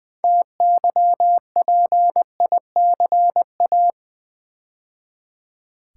Morse code, heard as a keyed tone, about 20 wpm, 700 Hz. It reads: TYPICA
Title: Build A Word - 3 Letters At A Time - 9-Letter Words - 20wpm